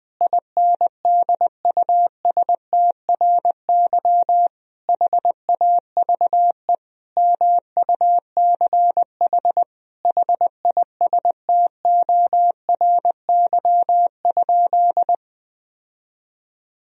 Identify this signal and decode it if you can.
Morse code, heard as a keyed tone, about 20 wpm, 700 Hz. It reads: INDUSTRY HAVE MUCH HISTORY?
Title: Sentences from Top 1000 Words - Review Part 4 - 20wpm